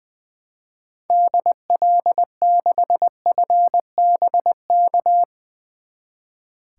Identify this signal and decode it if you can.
Morse code, heard as a keyed tone, about 21 wpm, 700 Hz. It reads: DL6FBK
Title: 2023 SOTA Activator Call Signs - 20wpm